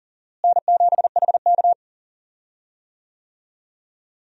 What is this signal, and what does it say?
Morse code, heard as a keyed tone, about 40 wpm, 700 Hz. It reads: N7HK
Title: USA Call Signs - 4 Characters - Repeated 2x - 40wpm